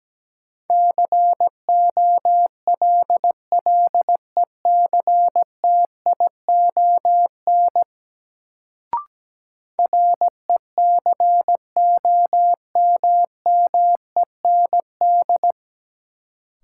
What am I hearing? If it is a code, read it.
Morse code, heard as a keyed tone, about 17 wpm, 700 Hz. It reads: COLLECTION RECOMMEND
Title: Sentences from Top 1200 Words - New Word Drill Alt - 17wpm